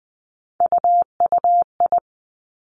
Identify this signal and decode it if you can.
Morse code, heard as a keyed tone, about 20 wpm, 700 Hz. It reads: UUI
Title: Three Challenging Letter-Number Combinations 20wpm